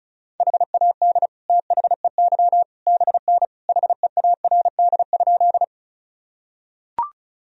Morse code, heard as a keyed tone, about 36 wpm, 700 Hz. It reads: HAD THEY BN HEARD?
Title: Sentences from Top 400 Words and Abbreviations 35wpm